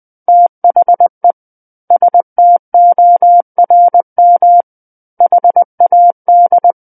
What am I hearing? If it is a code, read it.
Morse code, heard as a keyed tone, about 20 wpm, 700 Hz. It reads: THE STORM HAD
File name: 20 WPM